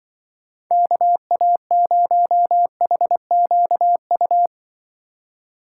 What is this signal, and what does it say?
Morse code, heard as a keyed tone, about 24 wpm, 700 Hz. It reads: KA0HQU
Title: Speed Racing - USA 2x3 Call Signs 20wpm